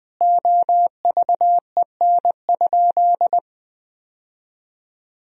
Morse code, heard as a keyed tone, about 21 wpm, 700 Hz. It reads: OVEN?
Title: Sentences from Top 1000 Words - Review Part 4 - 20wpm